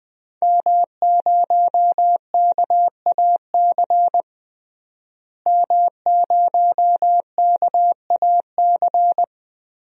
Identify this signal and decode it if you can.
Morse code, heard as a keyed tone, about 21 wpm, 700 Hz. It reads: M0KAC M0KAC
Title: DXCC-100 Call Signs - 5 Characters - Repeated 3x - 20wpm